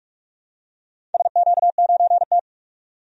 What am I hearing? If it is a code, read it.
Morse code, heard as a keyed tone, about 47 wpm, 700 Hz. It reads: SQ9T
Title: DXCC-100 Call Signs - 4 Characters - 45wpm